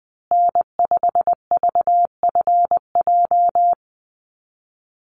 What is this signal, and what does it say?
Morse code, heard as a keyed tone, about 20 wpm, 700 Hz. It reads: N5VFJ
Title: USA Call Signs Repeated 3x 20wpm